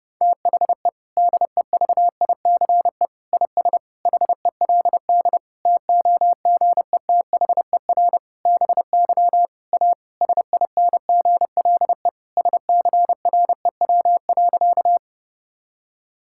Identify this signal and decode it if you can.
Morse code, heard as a keyed tone, about 30 wpm, 700 Hz. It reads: THE DEVICE IS HELD TOGETHER BY A SINGLE SCREW.
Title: Sentences from Top 1600 Words 30wpm